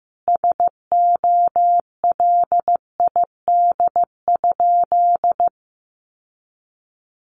Morse code, heard as a keyed tone, about 15 wpm, 700 Hz. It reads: SOLID?
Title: Sentences from Top 1500 Words 15wpm